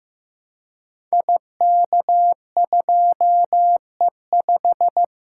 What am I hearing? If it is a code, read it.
Morse code, heard as a keyed tone, about 15 wpm, 700 Hz. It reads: IK2E5
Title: Code Groups - 5 Letter:Numbers - 15wpm